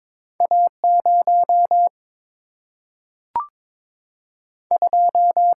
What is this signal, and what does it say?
Morse code, heard as a keyed tone, about 22 wpm, 700 Hz. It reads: A 0 2
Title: Single-Letter Single-Number 22wpm